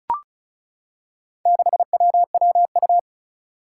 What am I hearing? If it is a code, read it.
Morse code, heard as a keyed tone, about 34 wpm, 700 Hz. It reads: 6WWU
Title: Four Challenging Letter-Number Combinations 35wpm